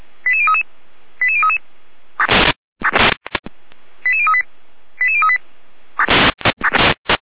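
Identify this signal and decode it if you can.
Five-tone selective call (ZVEI), heard as quick runs of five tones, tones of about 70 ms. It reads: 88033 88033 88038 88038
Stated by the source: ZVEI five-tone call 89354 received by SDR.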